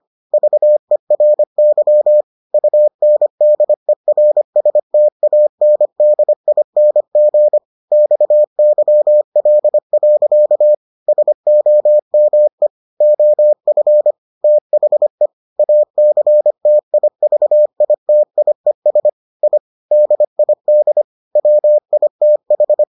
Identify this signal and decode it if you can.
Morse code, heard as a keyed tone, about 25 wpm, 600 Hz. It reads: VERY UNDERSTANDING XYL. SOME OF THE ACTIVITIES I DID WITH